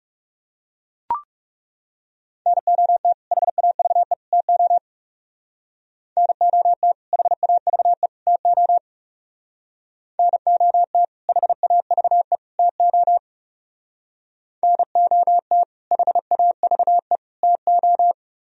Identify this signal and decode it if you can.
Morse code, heard as a keyed tone, about 37 wpm, 700 Hz. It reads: NOT HAVE TO NOT HAVE TO NOT HAVE TO NOT HAVE TO
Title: Speed-Racing - Sets of 3 Words 30wpm